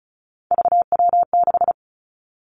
Morse code, heard as a keyed tone, about 36 wpm, 700 Hz. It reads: VW6
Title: Three Challenging Letter-Number Combinations 35wpm